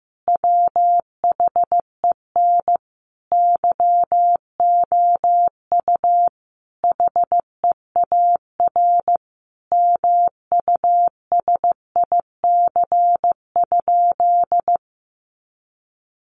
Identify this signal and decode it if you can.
Morse code, heard as a keyed tone, about 15 wpm, 700 Hz. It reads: WHEN YOU HEAR MUSIC?
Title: Sentences from Top 300 Words 15wpm